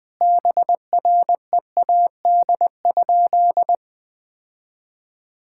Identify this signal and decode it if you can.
Morse code, heard as a keyed tone, about 20 wpm, 700 Hz. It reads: BREAD?
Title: Sentences from Top 1600 Words 20wpm